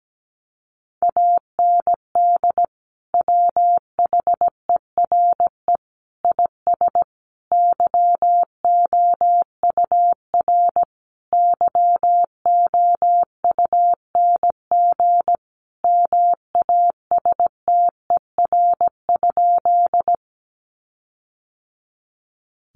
Morse code, heard as a keyed tone, about 17 wpm, 700 Hz. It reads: AND WHERE IS YOUR YOUNG MASTER?
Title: Sentences from Top 1300 Words 17wpm